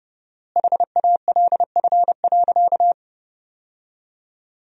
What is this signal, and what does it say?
Morse code, heard as a keyed tone, about 30 wpm, 700 Hz. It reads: HALF.
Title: 4x Word-Spacing - Sentences from Top 400 Words - 30wpm